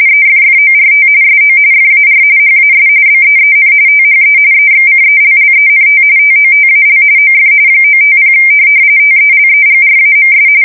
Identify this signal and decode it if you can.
RTTY, about 45.45 baud. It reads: EFFORT SO FAR HAS BEEN THE EXCELLENT TOTAL OF 950 TELEGRAMS
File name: fsk170 / morkrum